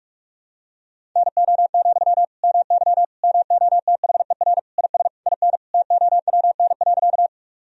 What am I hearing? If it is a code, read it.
Morse code, heard as a keyed tone, about 46 wpm, 700 Hz. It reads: NO, MY MOTHER IS IN TOWN.
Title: Sentences from Top 1000 Words - Review Part 2 - 45wpm